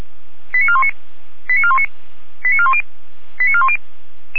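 Five-tone selective call (ZVEI), heard as quick runs of five tones, tones of about 70 ms. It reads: 89419 89419 89410 89410
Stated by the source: ZVEI five-tone call received by SDR.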